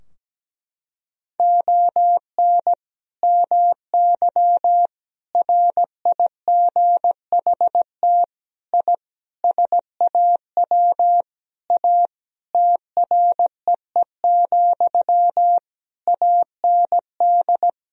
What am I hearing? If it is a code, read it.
Morse code, heard as a keyed tone, about 17 wpm, 700 Hz. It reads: ON MY RIGHT I SAW A TREE, AND
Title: Sentences from Top 1000 Words - Review Part 2 - 17wpm